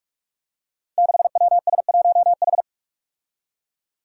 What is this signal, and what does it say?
Morse code, heard as a keyed tone, about 44 wpm, 700 Hz. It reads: 6WS1H
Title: Five Challenging Letter-Number Combinations 45wpm